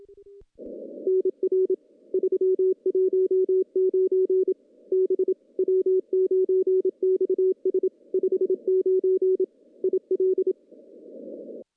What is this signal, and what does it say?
Morse code, heard as a keyed tone, about 27 wpm, 410 Hz. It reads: NR 319 B W9XS 59 IL
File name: contests / CW-SS-k5tr- -0700